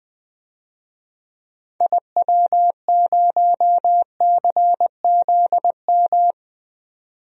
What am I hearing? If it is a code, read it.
Morse code, heard as a keyed tone, about 20 wpm, 700 Hz. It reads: IW0CZM
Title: DXCC-100 Call Signs - 6 Characters - 20wpm